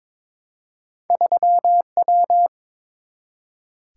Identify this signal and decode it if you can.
Morse code, heard as a keyed tone, about 22 wpm, 700 Hz. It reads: 3W
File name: practice / 22wpm / Two Letter-Number Combinations - 22wpm